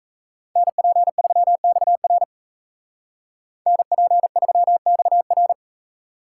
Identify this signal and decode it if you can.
Morse code, heard as a keyed tone, about 33 wpm, 700 Hz. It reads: NP3XR NP3XR
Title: Speed Racing - USA Call Signs 28wpm